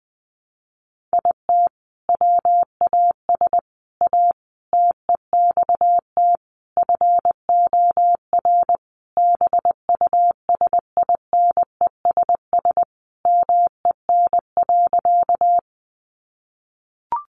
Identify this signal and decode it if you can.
Morse code, heard as a keyed tone, about 20 wpm, 700 Hz. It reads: IT WAS A TEXT FOR BUSINESS MEN.
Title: Sentences from Top 1200 Words 20wpm